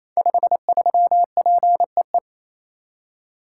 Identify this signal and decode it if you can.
Morse code, heard as a keyed tone, about 28 wpm, 700 Hz. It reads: 53PEE
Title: Code Groups - 5 Letter:Numbers - 28wpm